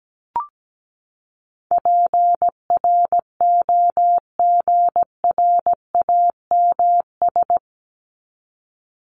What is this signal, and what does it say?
Morse code, heard as a keyed tone, about 17 wpm, 700 Hz. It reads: PROGRAMS
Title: ES:S Suffix - Top 100 Words with Base Words 17wpm